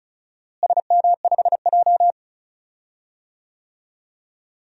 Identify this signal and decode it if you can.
Morse code, heard as a keyed tone, about 36 wpm, 700 Hz. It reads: SM5J
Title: DXCC-100 Call Signs - 4 Characters - Repeated 3x - 35wpm